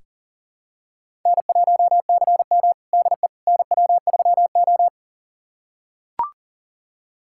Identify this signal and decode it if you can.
Morse code, heard as a keyed tone, about 40 wpm, 700 Hz. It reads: N1CM DE NW3O
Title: USA Callsign DE Callsign - 4 Characters - 40wpm